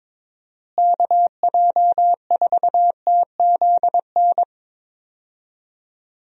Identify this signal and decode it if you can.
Morse code, heard as a keyed tone, about 21 wpm, 700 Hz. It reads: KJ4TZN
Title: USA Call Signs - 6 Characters - Repeated 2x - 22wpm